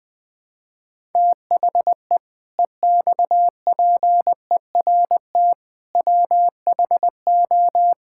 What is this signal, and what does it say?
Morse code, heard as a keyed tone, about 20 wpm, 700 Hz. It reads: THE EXPERT WHO
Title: Sentences with 7-Word Phrases - 20wpm